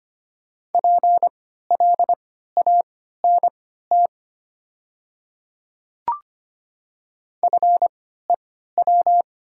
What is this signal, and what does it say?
Morse code, heard as a keyed tone, about 25 wpm, spaced as for 14 wpm, 690 Hz. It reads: PLANT FEW
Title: 3x Character Spacing - Top 300 Words - 25wpm